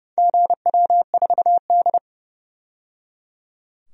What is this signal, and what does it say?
Morse code, heard as a keyed tone, about 30 wpm, 700 Hz. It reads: GW4D
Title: DXCC-100 Call Signs - 4 Characters - 30wpm